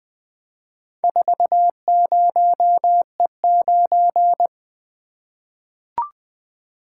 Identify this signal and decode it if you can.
Morse code, heard as a keyed tone, about 20 wpm, 700 Hz. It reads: 40E9
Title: Four Letter-Number Combinations - 20wpm